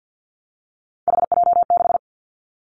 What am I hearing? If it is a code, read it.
Morse code, heard as a keyed tone, about 50 wpm, 700 Hz. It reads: HJ6
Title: Callsign Prefixes for Top 100 DXCC - 50wpm